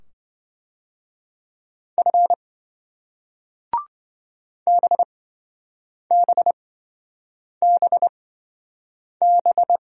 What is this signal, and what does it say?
Morse code, heard as a keyed tone, about 25 wpm, spaced as for 7 wpm, 700 Hz. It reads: F BBBB